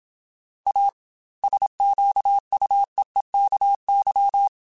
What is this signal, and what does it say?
Morse code, heard as a keyed tone, about 27 wpm, 800 Hz. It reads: A SQUEEKY
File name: Chuckles / Chuckles-mp3s / Squeaky @25